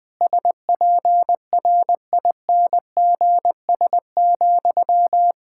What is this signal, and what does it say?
Morse code, heard as a keyed tone, about 20 wpm, 700 Hz. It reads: SPRINGS,
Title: Practice QSO - Exchange 1 - Beginning - 20wpm